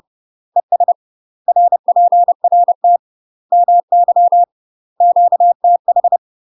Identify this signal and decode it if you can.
Morse code, heard as a keyed tone, about 30 wpm, 700 Hz. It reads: ES RPRT MY QTH
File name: UK MASTER QSOs TRAINING / 30WPM / UK MASTER QSOs 0046